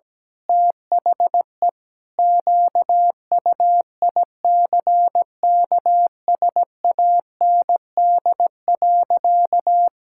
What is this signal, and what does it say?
Morse code, heard as a keyed tone, about 17 wpm, 700 Hz. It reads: THE QUICKSAND.
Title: Pangram Challenge v2 - 17wpm